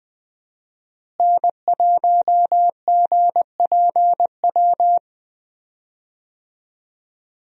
Morse code, heard as a keyed tone, about 20 wpm, 700 Hz. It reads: N1GPW
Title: USA Call Signs - 5 Characters - Repeated 2x - 20wpm